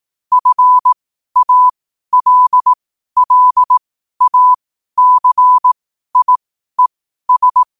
Morse code, heard as a keyed tone, about 18 wpm, spaced as for 12 wpm, 1000 Hz. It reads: FALLACIES
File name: 12wpm